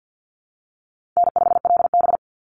Morse code, heard as a keyed tone, about 50 wpm, 700 Hz. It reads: N5LB
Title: Easy Call Signs Repeated 3x - 50wpm